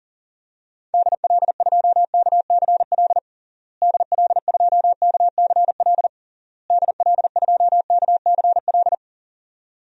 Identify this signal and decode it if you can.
Morse code, heard as a keyed tone, about 37 wpm, 700 Hz. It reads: DL2KCL DL2KCL DL2KCL
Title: DXCC-100 Call Signs - 6 Characters - Repeated 3x - 40wpm